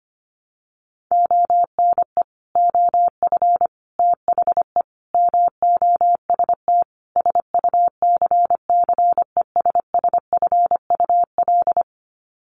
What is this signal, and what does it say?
Morse code, heard as a keyed tone, about 25 wpm, 700 Hz. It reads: ONE OF THE MOST SUCCESSFUL
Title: Sets of 5 Words Encore2 25wpm